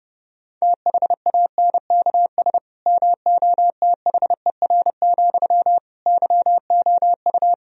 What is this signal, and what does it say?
Morse code, heard as a keyed tone, about 30 wpm, 700 Hz. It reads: THANKS MOTHER, YOU
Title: Sentences from Top 1000 Words - Review Part 3 - 30wpm